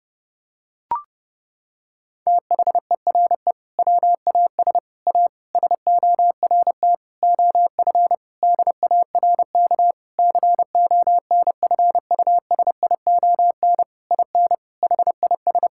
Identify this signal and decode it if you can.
Morse code, heard as a keyed tone, about 30 wpm, 700 Hz. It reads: THERE WAS A SORT OF DARK CONFUSION IN HIS
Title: Sentences from Top 1600 Words 30wpm